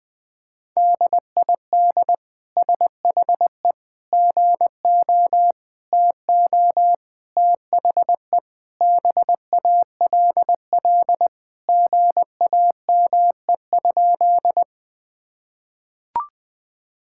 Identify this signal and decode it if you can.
Morse code, heard as a keyed tone, about 20 wpm, 700 Hz. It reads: DID SHE GO TO THE BALL GAME?
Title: Sentences from Top 1000 Words - Review Part 6 - 20wpm